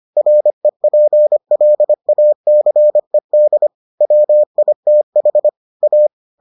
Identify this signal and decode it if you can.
Morse code, heard as a keyed tone, about 25 wpm, 600 Hz. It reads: REPLACED WITH A